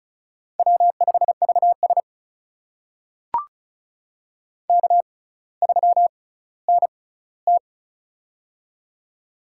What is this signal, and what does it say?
Morse code, heard as a keyed tone, about 36 wpm, spaced as for 18 wpm, 700 Hz. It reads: W5VS K3NT
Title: USA Call Signs - 4 Characters - 6x Character Spacing - 35wpm